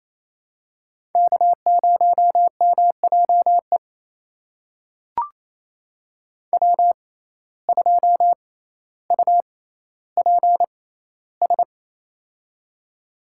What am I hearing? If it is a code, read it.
Morse code, heard as a keyed tone, about 28 wpm, spaced as for 14 wpm, 700 Hz. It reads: K0MJE W2UPS
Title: USA Call Signs - 5 Characters - 6x Character Spacing - 28wpm